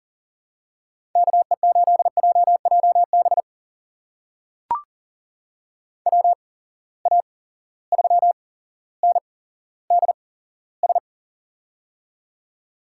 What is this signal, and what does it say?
Morse code, heard as a keyed tone, about 38 wpm, spaced as for 17 wpm, 700 Hz. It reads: KE8JJB WA3NDS
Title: USA Call Signs - 6 Characters - 8x Character Spacing - 40wpm